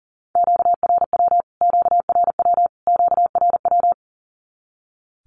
Morse code, heard as a keyed tone, about 40 wpm, 700 Hz. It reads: QRW QRW QRW
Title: Q-Codes Repeated 3x 40wpm